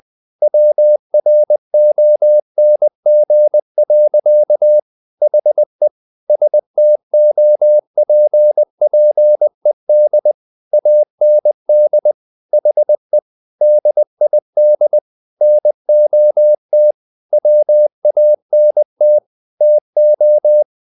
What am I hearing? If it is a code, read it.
Morse code, heard as a keyed tone, about 20 wpm, 600 Hz. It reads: WRONG. HE STOPPED AND HE DID NOT WANT TO
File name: the old man and the sea part 2 20wpm